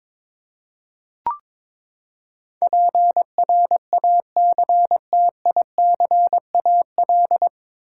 Morse code, heard as a keyed tone, about 22 wpm, 700 Hz. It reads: PRACTICAL
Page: Morse Code Ninja practice 22wpm